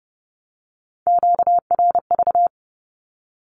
Morse code, heard as a keyed tone, about 30 wpm, 700 Hz. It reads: QRV
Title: Q-Codes Repeated 3x 30wpm